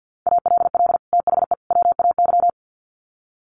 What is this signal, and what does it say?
Morse code, heard as a keyed tone, about 50 wpm, 700 Hz. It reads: ALL THE WAY
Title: Sets of 3 Words 50wpm